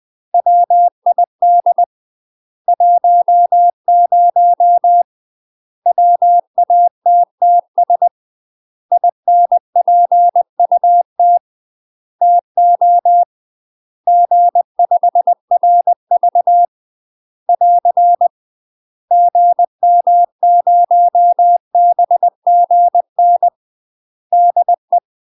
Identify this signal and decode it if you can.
Morse code, heard as a keyed tone, about 20 wpm, 700 Hz. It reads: WID 10 WATTS INPUT TO G5RV <AR> GM0BGN DE
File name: UK MASTER QSOs TRAINING / 20WPM / UK MASTER QSOs 0057